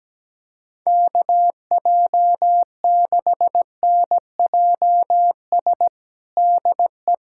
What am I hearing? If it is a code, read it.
Morse code, heard as a keyed tone, about 17 wpm, 700 Hz. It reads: KJ6NJS DE
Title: USA Callsign DE Callsign 17wpm